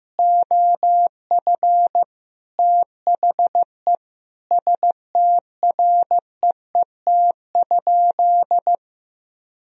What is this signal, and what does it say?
Morse code, heard as a keyed tone, about 15 wpm, 700 Hz. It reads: OF THE STREET?
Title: Sentences from Top 1000 Words - Review Part 3 - 15wpm